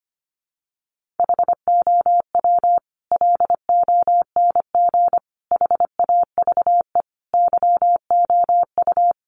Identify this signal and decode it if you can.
Morse code, heard as a keyed tone, about 25 wpm, 700 Hz. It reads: HOW LONG HAVE YOU
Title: Sets of 5 Words Encore 25wpm